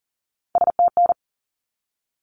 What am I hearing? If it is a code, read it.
Morse code, heard as a keyed tone, about 40 wpm, 700 Hz. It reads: STN
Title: Abbreviations 40wpm